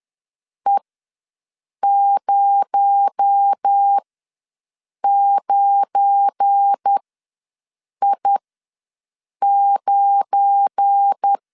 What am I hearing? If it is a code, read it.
Morse code, heard as a keyed tone, about 10 wpm, spaced as for 6 wpm, 800 Hz. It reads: E09I9
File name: LEZIONI / 7 GF9Z / 7 GF9 ws 10wpm 4034